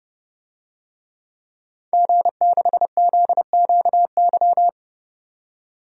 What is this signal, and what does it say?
Morse code, heard as a keyed tone, about 31 wpm, 700 Hz. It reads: G6ZQY
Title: DXCC-100 Call Signs - 5 Characters - 30wpm